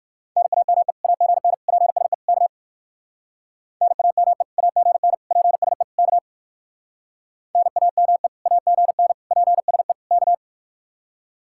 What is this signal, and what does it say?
Morse code, heard as a keyed tone, about 53 wpm, 700 Hz. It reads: NAME AGN PSE K NAME AGN PSE K NAME AGN PSE K
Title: Speed-Racing Common QSO Phrases Encore - 45wpm